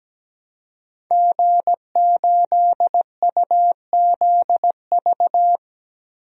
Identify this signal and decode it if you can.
Morse code, heard as a keyed tone, about 17 wpm, 700 Hz. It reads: G8UZV
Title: DXCC-100 Call Signs - 5 Characters - 17wpm